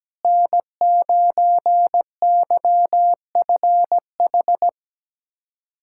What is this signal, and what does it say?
Morse code, heard as a keyed tone, about 17 wpm, 700 Hz. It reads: N9YFH
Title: USA Callsign DE Callsign - 5 Characters - 17wpm